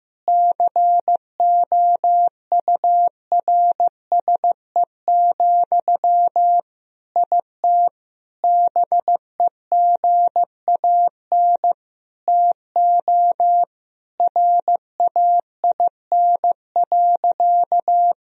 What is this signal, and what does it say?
Morse code, heard as a keyed tone, about 15 wpm, 700 Hz. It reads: COURSE, IT BEGAN TO RAIN.
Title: Sentences from Top 1000 Words - Review Part 1 - 15wpm